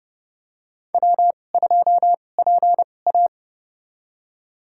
Morse code, heard as a keyed tone, about 30 wpm, spaced as for 22 wpm, 700 Hz. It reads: W2PA
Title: USA Call Signs - 4 Characters - 2x Character Spacing - 30wpm